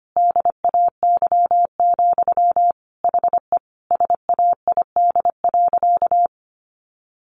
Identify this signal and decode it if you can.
Morse code, heard as a keyed tone, about 25 wpm, 700 Hz. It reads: DAY, HE SAID.
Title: Sentences from Top 900 Words 25wpm